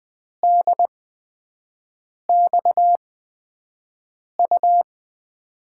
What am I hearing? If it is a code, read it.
Morse code, heard as a keyed tone, about 20 wpm, spaced as for 5 wpm, 700 Hz. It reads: DXU
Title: USA Call Signs - 8x Character Spacing - 20wpm